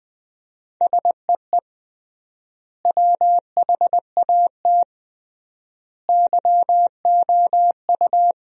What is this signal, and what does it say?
Morse code, heard as a keyed tone, about 20 wpm, 700 Hz. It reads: SEE WHAT YOU
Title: 3x Word-Spacing - Sentences from Top 500 Words - 20wpm